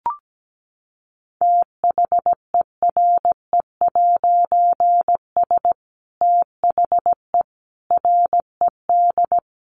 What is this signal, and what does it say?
Morse code, heard as a keyed tone, about 17 wpm, 700 Hz. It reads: THERE'S THE RED
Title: A Sound Of Thunder 17wpm